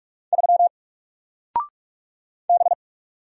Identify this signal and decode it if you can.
Morse code, heard as a keyed tone, about 44 wpm, 700 Hz. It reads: 3 B
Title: Single Character (Letter:Number:Punctuation:Prosign) Alt 45wpm